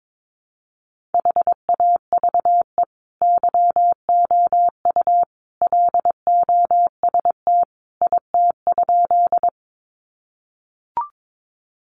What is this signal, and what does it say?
Morse code, heard as a keyed tone, about 22 wpm, 700 Hz. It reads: HAVE YOU LOST IT?
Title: Sentences from Top 1400 Words 22wpm